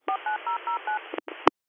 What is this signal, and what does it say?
DTMF, heard as telephone touch tones, tones of about 120 ms. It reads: 49009